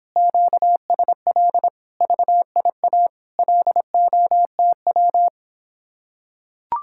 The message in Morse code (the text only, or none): QSL VIA LOTW